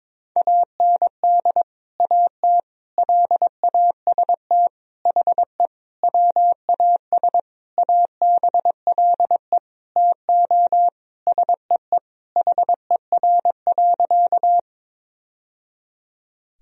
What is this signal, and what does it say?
Morse code, heard as a keyed tone, about 22 wpm, 700 Hz. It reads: AND AT LAST HE WAS ABLE TO SEE HER.
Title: Sentences from Top 1000 Words - Review Part 2 - 22wpm